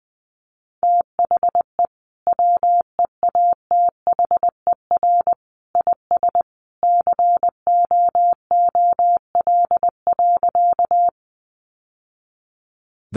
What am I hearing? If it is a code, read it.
Morse code, heard as a keyed tone, about 20 wpm, 700 Hz. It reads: THE WEATHER IS COOL.
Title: Sentences from Top 1200 Words 20wpm